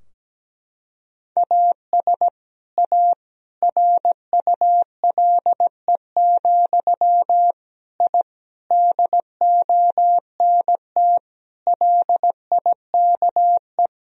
Morse code, heard as a keyed tone, about 17 wpm, 700 Hz. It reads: AS A RULE, I DONT LIKE
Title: Sentences from Top 1000 Words - Review Part 2 - 17wpm